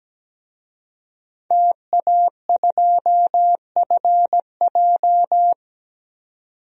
Morse code, heard as a keyed tone, about 17 wpm, 700 Hz. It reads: TA2FJ
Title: DXCC-100 Call Signs - 5 Characters - 17wpm